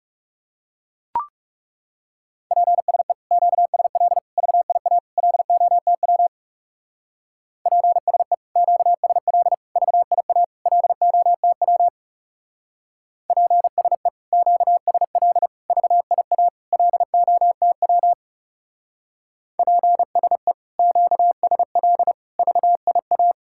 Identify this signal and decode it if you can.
Morse code, heard as a keyed tone, about 38 wpm, 700 Hz. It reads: PSE QSL VIA LOTW PSE QSL VIA LOTW PSE QSL VIA LOTW PSE QSL VIA